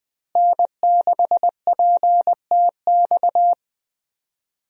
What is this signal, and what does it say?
Morse code, heard as a keyed tone, about 20 wpm, 700 Hz. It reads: N6PTX
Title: USA Call Signs - 5 Characters - Repeated 3x - 20wpm